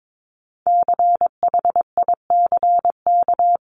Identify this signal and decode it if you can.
Morse code, heard as a keyed tone, about 22 wpm, 700 Hz. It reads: CHICK
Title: CWT Exchange 22wpm